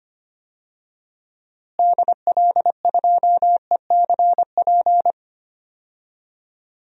Morse code, heard as a keyed tone, about 25 wpm, 700 Hz. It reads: DL2ECP
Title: DXCC-100 Call Signs - 6 Characters - Repeated 3x - 25wpm